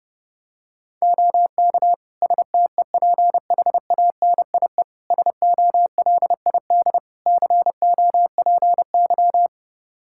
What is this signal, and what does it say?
Morse code, heard as a keyed tone, about 32 wpm, 700 Hz. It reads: OK STEPHANIE SOLID COPY
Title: Practice QSO - Exchange 3 - More Info - 30wpm